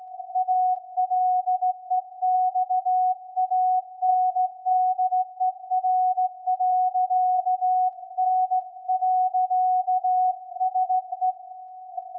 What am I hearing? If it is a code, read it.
Morse code, heard as a keyed tone, about 16 wpm, 740 Hz. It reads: ALEXANDER.N.SE
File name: iz7slz / VLF / SAQ 24-12-2021